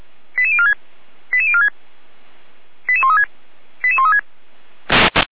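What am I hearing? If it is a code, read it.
Five-tone selective call (ZVEI), heard as quick runs of five tones, tones of about 70 ms. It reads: 88046 88046 88136 88136
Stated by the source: ZVEI five-tone call 88402 received by SDR, from the Feuerwehr Orlamünde fire brigade audio archive.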